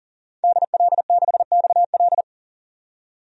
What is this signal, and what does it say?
Morse code, heard as a keyed tone, about 38 wpm, 700 Hz. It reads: DL6XL
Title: DXCC-100 Call Signs - 5 Characters - 40wpm